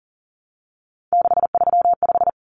Morse code, heard as a keyed tone, about 40 wpm, 700 Hz. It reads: 635
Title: Common Cut Numbers - 3-Digit 40wpm